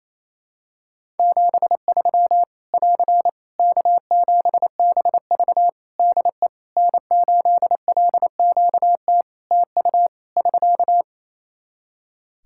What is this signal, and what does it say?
Morse code, heard as a keyed tone, about 27 wpm, 700 Hz. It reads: 73 <AR> K7BV DE N8LQT TU <SK>
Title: Practice QSO - Exchange 4 - Ending - 28wpm